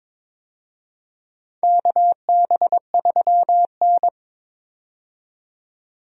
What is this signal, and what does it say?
Morse code, heard as a keyed tone, about 22 wpm, 700 Hz. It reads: KB3N
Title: USA Call Signs - 4 Characters - 22wpm